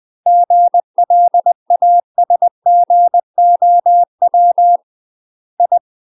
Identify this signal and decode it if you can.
Morse code, heard as a keyed tone, about 20 wpm, 700 Hz. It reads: GLASGOW I
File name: UK MASTER QSOs TRAINING / 20WPM / UK MASTER QSOs 0066